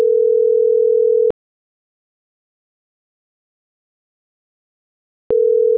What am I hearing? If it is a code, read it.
Ringback tone (North America), 440 + 480 Hz.